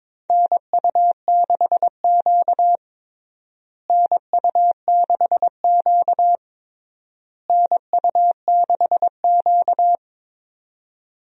Morse code, heard as a keyed tone, about 22 wpm, 700 Hz. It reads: NU6Q NU6Q NU6Q
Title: USA Call Signs - 4 Characters - Repeated 3x - 22wpm